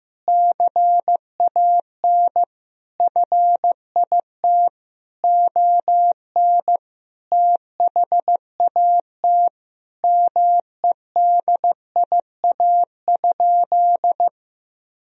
Morse code, heard as a keyed tone, about 15 wpm, 700 Hz. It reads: CAN FIT ON THAT MEDIA?
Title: Sentences from Top 1000 Words - Review Part 4 - 15wpm